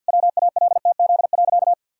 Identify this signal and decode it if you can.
Morse code, heard as a keyed tone, about 52 wpm, 700 Hz. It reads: WALTZ.
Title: Pangram Challenge v2 - 50wpm